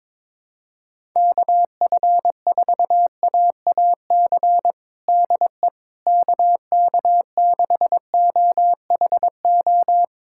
Morse code, heard as a keyed tone, about 22 wpm, 700 Hz. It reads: KF4AAC DE KK6OHO
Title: USA Callsign DE Callsign 22wpm